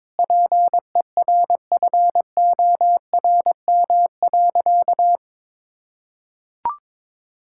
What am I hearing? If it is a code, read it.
Morse code, heard as a keyed tone, about 21 wpm, 700 Hz. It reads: PERFORM.
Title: Sentences from Top 1000 Words - Review Part 5 - 22wpm